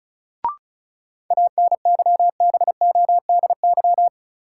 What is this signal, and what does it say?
Morse code, heard as a keyed tone, about 36 wpm, 700 Hz. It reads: ANYBODY
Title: Sets of 5 Words - New Word Drill - 35wpm